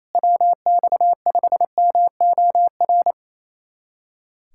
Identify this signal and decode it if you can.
Morse code, heard as a keyed tone, about 27 wpm, 700 Hz. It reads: WX5MOR
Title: USA Callsign DE Callsign - 6 Characters - 28wpm